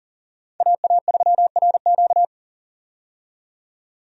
Morse code, heard as a keyed tone, about 40 wpm, 700 Hz. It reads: AA3RQ
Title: USA Call Signs - 5 Characters - 40wpm